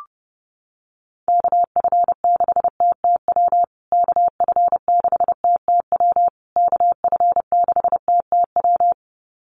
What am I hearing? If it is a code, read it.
Morse code, heard as a keyed tone, about 30 wpm, 700 Hz. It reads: KF6TTW KF6TTW KF6TTW
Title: USA Call Signs Repeated 3x 30wpm